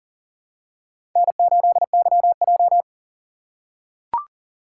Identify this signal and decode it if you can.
Morse code, heard as a keyed tone, about 37 wpm, 700 Hz. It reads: N8YJ
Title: USA Call Signs - 4 Characters - 3x Character Spacing - 40wpm